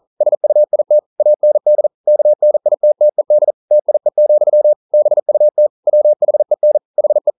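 Morse code, heard as a keyed tone, about 42 wpm, 600 Hz. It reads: SUIT AND KNITTED TIE, BUT WHEN HE
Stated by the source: Morse code practice at 41 wpm, from CW med Gnister.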